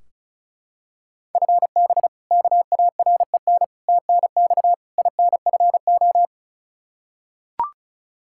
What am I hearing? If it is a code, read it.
Morse code, heard as a keyed tone, about 36 wpm, 700 Hz. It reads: FB KAREN TNX INFO
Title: K1USN SST QSO Practice 35wpm